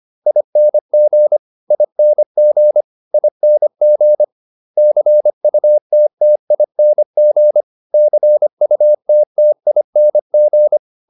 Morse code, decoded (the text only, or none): ING ING ING CUTTING CUTTING